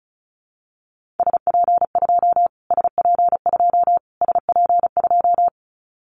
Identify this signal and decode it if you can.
Morse code, heard as a keyed tone, about 34 wpm, 700 Hz. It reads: SP2 SP2 SP2
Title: Callsign Prefixes for Top 25 DXCC - Repeated 3x - 35wpm